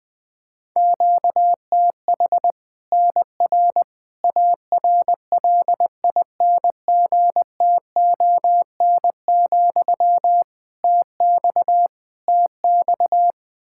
Morse code, heard as a keyed tone, about 20 wpm, 700 Hz. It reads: QTH NR ARLINGTON, TX TX
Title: K1USN SST QSO Practice 20wpm